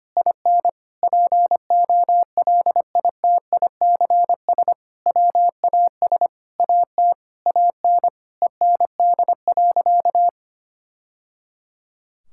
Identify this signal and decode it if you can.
Morse code, heard as a keyed tone, about 25 wpm, 700 Hz. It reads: IN POLITICS WAS AT AN END.
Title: Sentences from Top 1000 Words - Review Part 6 - 25wpm